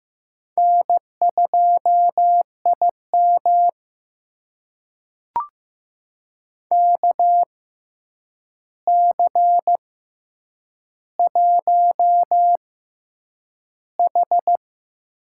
Morse code, heard as a keyed tone, about 15 wpm, spaced as for 8 wpm, 700 Hz. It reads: N2IM KC1H
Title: USA Call Signs - 4 Characters - 6x Character Spacing - 15wpm